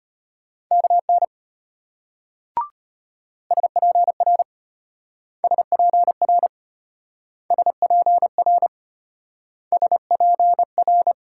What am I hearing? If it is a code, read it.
Morse code, heard as a keyed tone, about 32 wpm, 700 Hz. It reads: KN SPR SPR SPR SPR